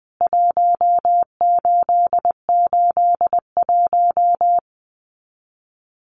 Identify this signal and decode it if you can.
Morse code, heard as a keyed tone, about 20 wpm, 700 Hz. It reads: 1881
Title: CWT Exchange 20wpm